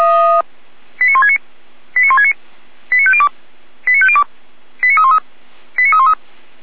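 Five-tone selective call (ZVEI), heard as quick runs of five tones, tones of about 70 ms. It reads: 89169 89169 89502 89502 89313 89313
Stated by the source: ZVEI five-tone selective call received by SDR, from the Feuerwehr Orlamünde fire brigade server.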